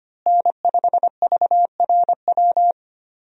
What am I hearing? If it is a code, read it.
Morse code, heard as a keyed tone, about 25 wpm, 700 Hz. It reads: N5VRW
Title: USA Call Signs - 5 Characters - Repeated 3x - 25wpm